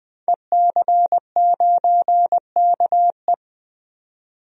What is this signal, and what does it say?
Morse code, heard as a keyed tone, about 20 wpm, 700 Hz. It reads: EC9KE
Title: DXCC-100 Call Signs - 5 Characters - 20wpm